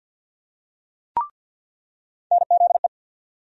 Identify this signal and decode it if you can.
Morse code, heard as a keyed tone, about 47 wpm, 700 Hz. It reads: NZE
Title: USA Call Sign Postfixes - 3 Letters 50wpm